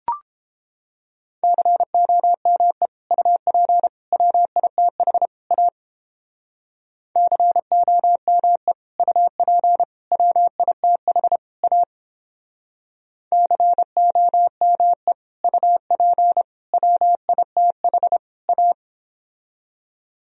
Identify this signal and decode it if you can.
Morse code, heard as a keyed tone, about 29 wpm, 700 Hz. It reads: COME UP WITH A COME UP WITH A COME UP WITH A